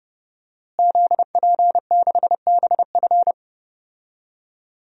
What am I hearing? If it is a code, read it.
Morse code, heard as a keyed tone, about 30 wpm, 700 Hz. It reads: ZP6BF
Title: DXCC-100 Call Signs - 5 Characters - Repeated 3x - 30wpm